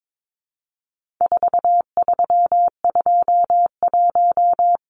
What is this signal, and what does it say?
Morse code, heard as a keyed tone, about 22 wpm, 700 Hz. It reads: 4321
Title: Numbers - 4-Digit 22wpm